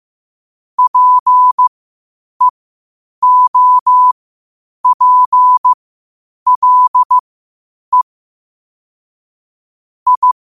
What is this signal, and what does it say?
Morse code, heard as a keyed tone, about 15 wpm, spaced as for 8 wpm, 1000 Hz. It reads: PEOPLE I
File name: qotdfiles / 8wpm